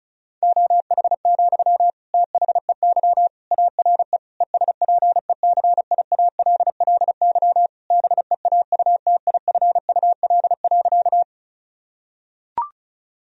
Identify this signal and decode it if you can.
Morse code, heard as a keyed tone, about 36 wpm, 700 Hz. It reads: OH, THEY ARE ESPECIALLY BEAUTIFUL.
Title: Sentences from Top 1000 Words - Review Part 4 - 35wpm